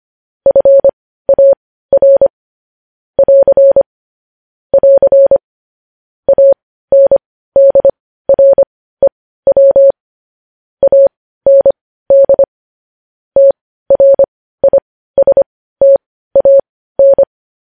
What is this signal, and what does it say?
Morse code, heard as a keyed tone, about 25 wpm, spaced as for 15 wpm, 550 Hz. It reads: FAR <AR> <AR> ANDREW AND TRISTAN
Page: News Headlines in Morse Code at 15 WPM